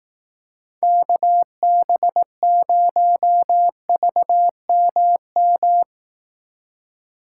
Morse code, heard as a keyed tone, about 18 wpm, 700 Hz. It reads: KB0VMM
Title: Speed Racing - USA Call Signs 15wpm